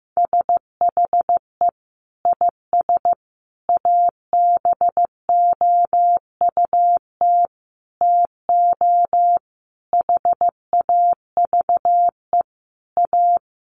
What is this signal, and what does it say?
Morse code, heard as a keyed tone, about 14 wpm, 700 Hz. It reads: SHE IS ABOUT TO HAVE A
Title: Sentences from Top 1300 Words 15wpm